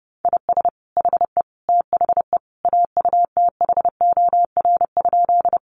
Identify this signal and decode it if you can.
Morse code, heard as a keyed tone, about 30 wpm, 700 Hz. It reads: IS HE THE AUTHOR?
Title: Sentences from Top 1200 Words 30wpm